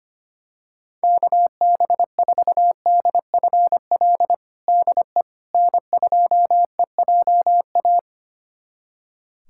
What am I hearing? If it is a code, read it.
Morse code, heard as a keyed tone, about 25 wpm, 700 Hz. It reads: KB4DFL DE N2EJA
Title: USA Callsign DE Callsign 25wpm